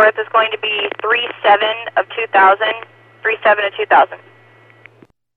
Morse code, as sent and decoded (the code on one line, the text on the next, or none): none